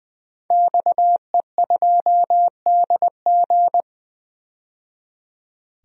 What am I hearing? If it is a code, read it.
Morse code, heard as a keyed tone, about 21 wpm, 700 Hz. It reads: XE2DG
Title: DXCC-100 Call Signs - 5 Characters - 20wpm